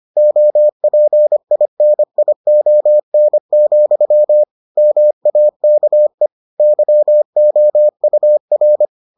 Morse code, heard as a keyed tone, about 25 wpm, 600 Hz. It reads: OPINION, MAKE YOUR